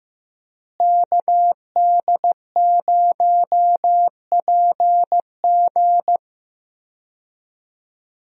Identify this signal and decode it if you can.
Morse code, heard as a keyed tone, about 15 wpm, 700 Hz. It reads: KD0PG
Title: USA Call Signs - 5 Characters - Repeated 3x - 15wpm